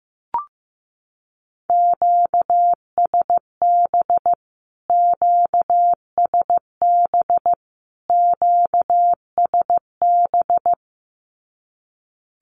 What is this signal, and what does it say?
Morse code, heard as a keyed tone, about 15 wpm, 700 Hz. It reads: QSB QSB QSB
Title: Q-Codes Repeated 3x 15wpm